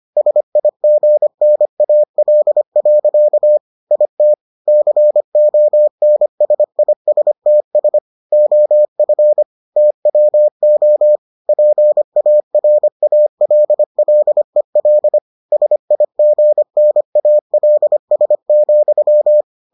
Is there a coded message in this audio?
Yes: Morse code from 0.2 to 19.4 s, heard as a keyed tone, about 25 wpm, 600 Hz. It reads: SIGNAL. IT CONSISTS OF TWO PARALLEL SIGNALS,